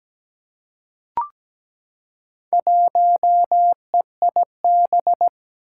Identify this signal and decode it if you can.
Morse code, heard as a keyed tone, about 17 wpm, 700 Hz. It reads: 1EIB
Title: Four Challenging Letter-Number Combinations 17wpm